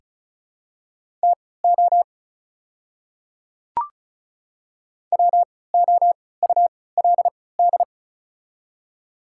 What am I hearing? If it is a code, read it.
Morse code, heard as a keyed tone, about 36 wpm, spaced as for 20 wpm, 690 Hz. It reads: TO WOULD
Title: 3x Character Spacing - Top 100 Words - 35wpm